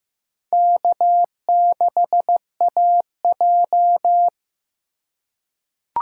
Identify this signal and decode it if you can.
Morse code, heard as a keyed tone, about 15 wpm, 700 Hz. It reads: K6AJ
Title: USA Call Signs - 4 Characters - 2x Character Spacing - 15wpm